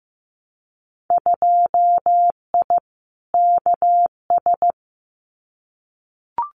Morse code, heard as a keyed tone, about 15 wpm, 700 Hz. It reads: 2I KS
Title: Winter Field Day Exchanges Encore 15wpm